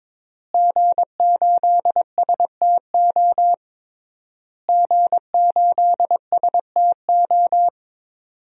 Morse code, heard as a keyed tone, about 22 wpm, 700 Hz. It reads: G8STO G8STO
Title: DXCC-100 Call Signs - 5 Characters - Repeated 3x - 22wpm